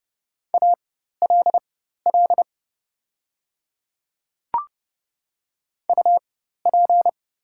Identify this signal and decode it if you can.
Morse code, heard as a keyed tone, about 30 wpm, spaced as for 14 wpm, 700 Hz. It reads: ALL UP
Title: 4x Character Spacing - Top 100 Words - 30wpm